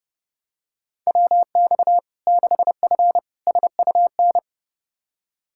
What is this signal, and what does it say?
Morse code, heard as a keyed tone, about 30 wpm, 700 Hz. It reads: WX 6F SUN
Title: Practice QSO - Exchange 2 - Info - 30wpm